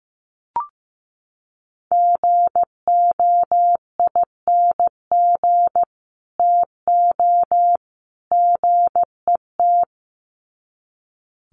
Morse code, heard as a keyed tone, about 15 wpm, 700 Hz. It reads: GOING TO GET
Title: Sets of 3 Words 15wpm